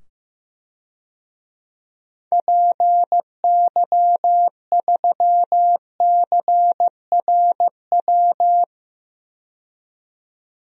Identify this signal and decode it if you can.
Morse code, heard as a keyed tone, about 15 wpm, 700 Hz. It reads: PY3CRW
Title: DXCC-100 Call Signs - 6 Characters - 15wpm